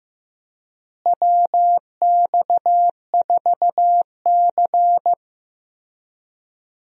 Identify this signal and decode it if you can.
Morse code, heard as a keyed tone, about 15 wpm, 700 Hz. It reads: WX4C
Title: USA Call Signs - 4 Characters - Repeated 2x - 15wpm